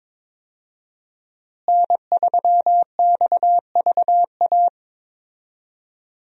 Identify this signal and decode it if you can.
Morse code, heard as a keyed tone, about 22 wpm, 700 Hz. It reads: N3XVA
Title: USA Call Signs - 5 Characters - Repeated 3x - 22wpm